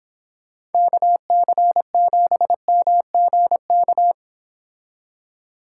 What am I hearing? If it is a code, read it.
Morse code, heard as a keyed tone, about 29 wpm, 700 Hz. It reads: KC7MGK